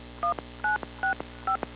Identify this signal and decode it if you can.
DTMF, heard as telephone touch tones, tones of about 120 ms. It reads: 1965